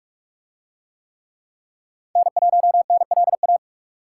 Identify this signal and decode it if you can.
Morse code, heard as a keyed tone, about 43 wpm, 700 Hz. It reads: N1NLA